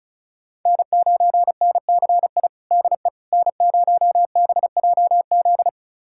Morse code, heard as a keyed tone, about 36 wpm, 700 Hz. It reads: N9NCI DE N0BJZ
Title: USA Callsign DE Callsign - 5 Characters - 35wpm